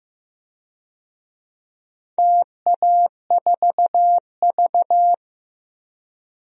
Morse code, heard as a keyed tone, about 15 wpm, 700 Hz. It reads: TA4V
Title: DXCC-100 Call Signs - 4 Characters - Repeated 3x - 15wpm